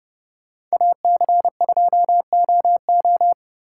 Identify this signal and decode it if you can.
Morse code, heard as a keyed tone, about 35 wpm, 700 Hz. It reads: AC2OO
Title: Speed Racing - USA 2x2 Call Signs 30wpm